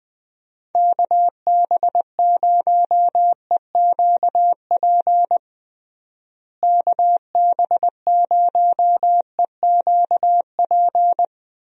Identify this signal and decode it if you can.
Morse code, heard as a keyed tone, about 20 wpm, 700 Hz. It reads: KB0EQP KB0EQP
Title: USA Call Signs - 6 Characters - Repeated 2x - 20wpm